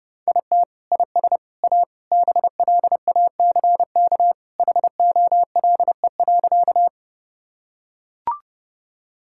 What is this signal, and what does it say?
Morse code, heard as a keyed tone, about 30 wpm, 700 Hz. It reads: IT IS A BLACK HOLE.
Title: Sentences from Top 1600 Words 30wpm